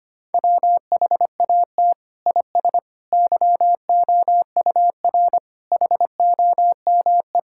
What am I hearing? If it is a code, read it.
Morse code, heard as a keyed tone, about 25 wpm, 700 Hz. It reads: WHAT IS YOUR HOME
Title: Sentences from Top 1000 Words - Review Part 6 - 25wpm